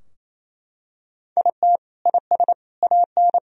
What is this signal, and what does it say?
Morse code, heard as a keyed tone, about 28 wpm, 700 Hz. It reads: IT IS AN
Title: Sentences from Top 1000 Words - Review Part 4 - 28wpm